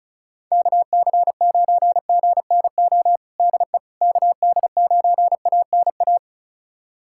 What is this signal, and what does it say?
Morse code, heard as a keyed tone, about 34 wpm, 700 Hz. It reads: KC9GNO DE KD9ANA
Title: USA Callsign DE Callsign - 6 Characters - 35wpm